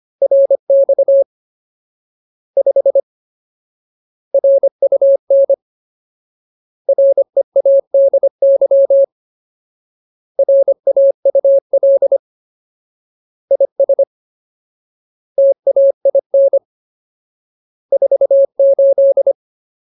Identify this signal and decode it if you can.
Morse code, heard as a keyed tone, about 25 wpm, 550 Hz. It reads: RX 5 RUN READY RAUL IS TAIN 48
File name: Ultimate Word List / 25WPM Ultimate